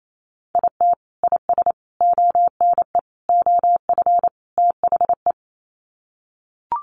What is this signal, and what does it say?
Morse code, heard as a keyed tone, about 28 wpm, 700 Hz. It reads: IT IS ONE OF THE
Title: Sets of 5 Words 28wpm